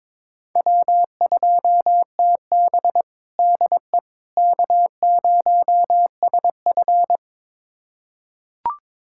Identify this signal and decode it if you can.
Morse code, heard as a keyed tone, about 22 wpm, 700 Hz. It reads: W2TB DE K0SF
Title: USA Callsign DE Callsign - 4 Characters - 22wpm